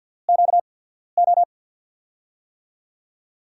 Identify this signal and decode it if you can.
Morse code, heard as a keyed tone, about 49 wpm, spaced as for 14 wpm, 700 Hz. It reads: QO
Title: USA Call Signs - 6 Characters - 8x Character Spacing - 50wpm